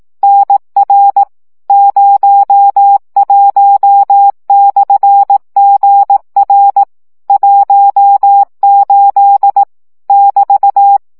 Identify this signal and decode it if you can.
Morse code, heard as a keyed tone, about 18 wpm, 800 Hz. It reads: NR 01/GR 18 <BT>
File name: morze vizsga 2011-1 15